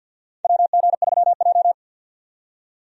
Morse code, heard as a keyed tone, about 54 wpm, 700 Hz. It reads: WG3J
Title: USA Call Signs - 4 Characters - 50wpm